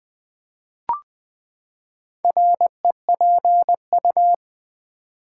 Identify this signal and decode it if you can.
Morse code, heard as a keyed tone, about 20 wpm, 700 Hz. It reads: REPU
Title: Build A Word - 4 Letters At A Time - 8-Letter Words - 20wpm